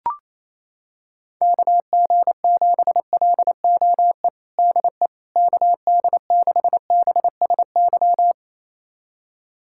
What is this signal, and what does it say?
Morse code, heard as a keyed tone, about 28 wpm, 700 Hz. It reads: KG7LOE DE KD6BSY
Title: USA Callsign DE Callsign - 6 Characters - 28wpm